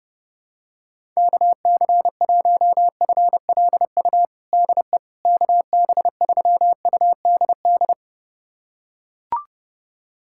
Morse code, heard as a keyed tone, about 32 wpm, 700 Hz. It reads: KC1FLU DE KB3UDD
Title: USA Callsign DE Callsign - 6 Characters - 30wpm